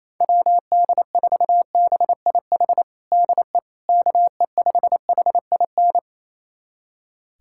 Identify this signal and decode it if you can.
Morse code, heard as a keyed tone, about 28 wpm, 700 Hz. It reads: WD4BIH DE KE5HIN
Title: USA Callsign DE Callsign 28wpm